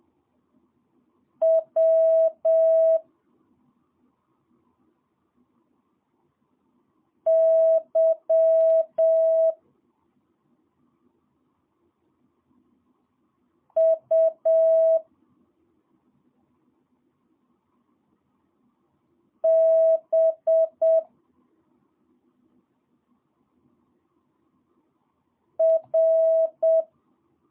Morse code, heard as a keyed tone, about 7 wpm, 650 Hz. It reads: W Y U B R
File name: jh8cbh / 2022cw